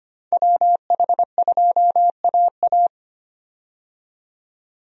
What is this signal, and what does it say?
Morse code, heard as a keyed tone, about 25 wpm, 700 Hz. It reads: WH2AA
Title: USA Call Signs - 5 Characters - 25wpm